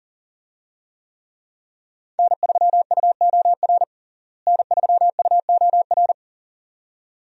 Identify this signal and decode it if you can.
Morse code, heard as a keyed tone, about 40 wpm, 700 Hz. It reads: N3UOR N3UOR
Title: USA Call Signs - 5 Characters - Repeated 2x - 40wpm